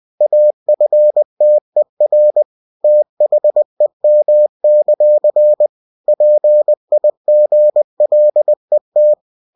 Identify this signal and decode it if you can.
Morse code, heard as a keyed tone, about 20 wpm, 600 Hz. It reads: AFTER THEM; PIGLET
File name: cw-books / winnie-the-pooh / winnie-20wpm-page-0022